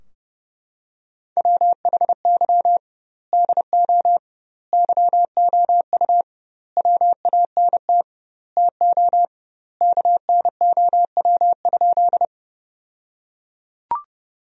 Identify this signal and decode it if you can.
Morse code, heard as a keyed tone, about 30 wpm, 700 Hz. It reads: WHY DO YOU WANT TO KNOW?
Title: 2x Word-Spacing - Sentences from Top 200 Words - 30wpm